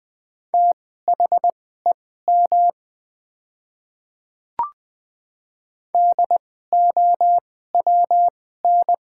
Morse code, heard as a keyed tone, about 20 wpm, spaced as for 14 wpm, 700 Hz. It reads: THEM DOWN
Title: 2x Character Spacing - Top 100 Words - 20wpm